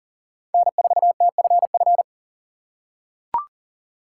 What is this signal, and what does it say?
Morse code, heard as a keyed tone, about 40 wpm, 700 Hz. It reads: N4TFF
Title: USA Call Signs - 3x Character Spacing - 40wpm